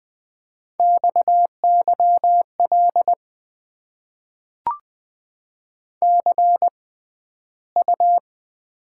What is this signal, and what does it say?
Morse code, heard as a keyed tone, about 20 wpm, spaced as for 12 wpm, 700 Hz. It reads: XYL CU
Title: Top 90 QSO Elements - 6x Character Spacing - 20wpm